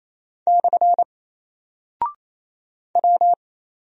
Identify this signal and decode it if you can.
Morse code, heard as a keyed tone, about 28 wpm, 700 Hz. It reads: / W
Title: Single Character (Letter:Number:Punctuation:Prosign) Alt 28wpm